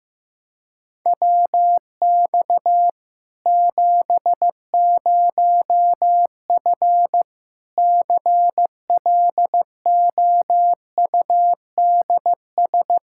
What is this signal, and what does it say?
Morse code, heard as a keyed tone, about 15 wpm, 700 Hz. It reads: WX 70F CLOUDS
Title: Practice QSO - Exchange 2 - Info - 15wpm